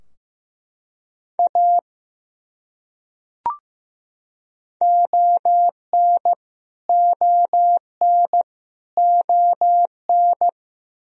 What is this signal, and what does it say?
Morse code, heard as a keyed tone, about 15 wpm, 700 Hz. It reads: A ON ON ON
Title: Top 100 Words Repeated 15wpm